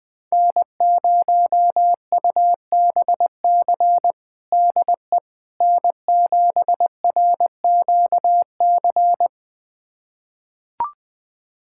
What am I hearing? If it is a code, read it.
Morse code, heard as a keyed tone, about 20 wpm, 700 Hz. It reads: N0UBC DE N7RQC
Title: USA Callsign DE Callsign - 5 Characters - 20wpm